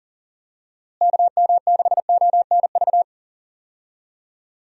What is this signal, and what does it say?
Morse code, heard as a keyed tone, about 40 wpm, 700 Hz. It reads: KM6ONV
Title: USA Call Signs - 6 Characters - Repeated 3x - 40wpm